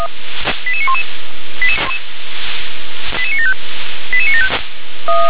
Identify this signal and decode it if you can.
Five-tone selective call (ZVEI), heard as quick runs of five tones, tones of about 70 ms. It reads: 88911 88975 88975
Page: /357/SDR/ZVEI/